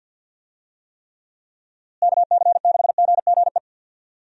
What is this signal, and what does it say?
Morse code, heard as a keyed tone, about 48 wpm, 700 Hz. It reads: KC6GGE
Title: USA Call Signs - 6 Characters - Repeated 3x - 50wpm